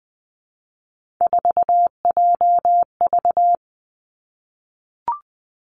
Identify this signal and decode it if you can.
Morse code, heard as a keyed tone, about 20 wpm, 700 Hz. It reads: 4JV
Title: Three Challenging Letter-Number Combinations 20wpm